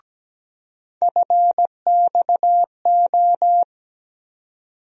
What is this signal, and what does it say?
Morse code, heard as a keyed tone, about 17 wpm, 700 Hz. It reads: FXO
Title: USA Call Sign Postfixes - 3 Letters 17wpm